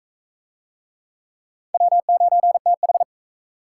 Morse code, heard as a keyed tone, about 35 wpm, 700 Hz. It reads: W9TH
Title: Speed Racing - USA Call Signs 28wpm